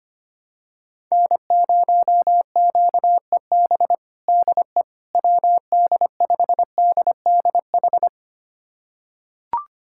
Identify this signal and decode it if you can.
Morse code, heard as a keyed tone, about 25 wpm, 700 Hz. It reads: N0QEB DE WD5DDH
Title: USA Callsign DE Callsign 25wpm